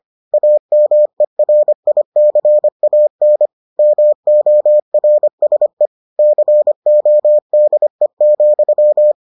Morse code, heard as a keyed tone, about 25 wpm, 600 Hz. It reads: AMERICAN MORSE CODE,